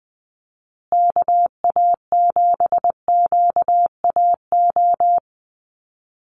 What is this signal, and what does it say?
Morse code, heard as a keyed tone, about 20 wpm, 700 Hz. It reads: KA7QAO
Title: USA Call Signs Repeated 2x 20wpm